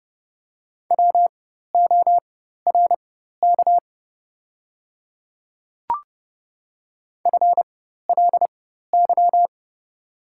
Morse code, heard as a keyed tone, about 29 wpm, spaced as for 14 wpm, 690 Hz. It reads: WORK FLY
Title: 4x Character Spacing - Top 500 Words - 30wpm